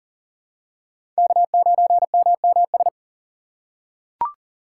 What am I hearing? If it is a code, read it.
Morse code, heard as a keyed tone, about 37 wpm, 700 Hz. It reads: K9MMS
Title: K1USN SST QSO Practice - 40wpm